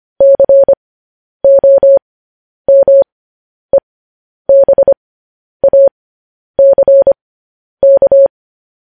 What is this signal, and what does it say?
Morse code, heard as a keyed tone, about 25 wpm, spaced as for 10 wpm, 550 Hz. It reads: COMEBACK
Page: news headlines in morse code at 10 wpm